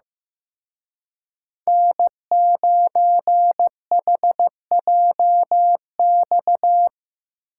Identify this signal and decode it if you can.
Morse code, heard as a keyed tone, about 17 wpm, 700 Hz. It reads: N9HJX
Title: Speed Racing - USA Call Signs 15wpm